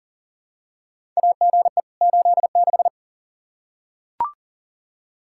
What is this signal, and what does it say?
Morse code, heard as a keyed tone, about 38 wpm, 700 Hz. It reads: AGE 86
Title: K1USN SST QSO Practice 40wpm